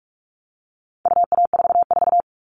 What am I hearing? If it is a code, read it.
Morse code, heard as a keyed tone, about 44 wpm, 700 Hz. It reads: UA44
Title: All Cut Numbers - 4-Digit 45wpm